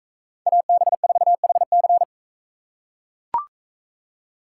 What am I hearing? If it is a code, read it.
Morse code, heard as a keyed tone, about 41 wpm, 700 Hz. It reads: AB4HC